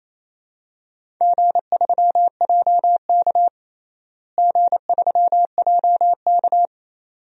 Morse code, heard as a keyed tone, about 28 wpm, 700 Hz. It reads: G3JK G3JK
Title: DXCC-100 Call Signs - 4 Characters - Repeated 2x - 28wpm